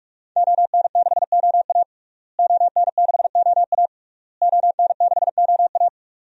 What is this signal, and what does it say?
Morse code, heard as a keyed tone, about 41 wpm, 700 Hz. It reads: ON6OA ON6OA ON6OA
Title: DXCC-100 Call Signs - 5 Characters - Repeated 3x - 45wpm